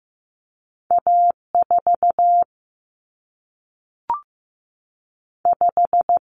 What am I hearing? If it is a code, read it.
Morse code, heard as a keyed tone, about 15 wpm, 700 Hz. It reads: A 4 5
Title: Common Cut Numbers - 2-Digit 15wpm